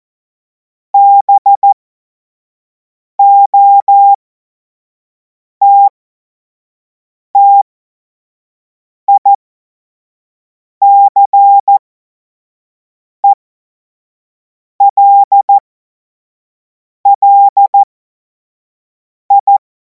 Morse code, heard as a keyed tone, about 14 wpm, spaced as for 5 wpm, 800 Hz. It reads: BOTTICELLI
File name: Morsekurs / Witze / Witz 31. März 2026 25 F70